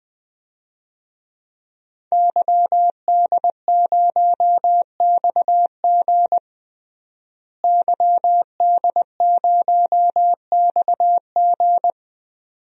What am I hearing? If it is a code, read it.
Morse code, heard as a keyed tone, about 20 wpm, 700 Hz. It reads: YD0XG YD0XG
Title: DXCC-100 Call Signs - 5 Characters - Repeated 2x - 20wpm